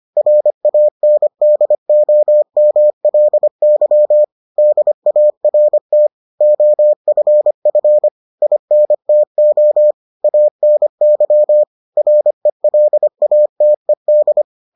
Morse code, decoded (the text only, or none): RANDOMLY DART OFF INTO ANY RELATED